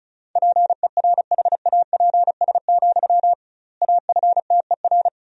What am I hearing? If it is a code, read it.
Morse code, heard as a keyed tone, about 34 wpm, 700 Hz. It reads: PERHAPS, AFTER
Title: 2x Word-Spacing - Sentences from Top 500 Words - 35wpm